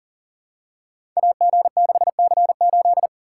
Morse code, heard as a keyed tone, about 42 wpm, 700 Hz. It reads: AG6C8
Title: Code Groups - 5 Letter:Numbers - 40wpm